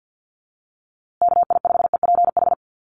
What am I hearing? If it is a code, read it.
Morse code, heard as a keyed tone, about 48 wpm, 700 Hz. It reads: KI5EPH
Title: USA Call Signs Repeated 2x 50wpm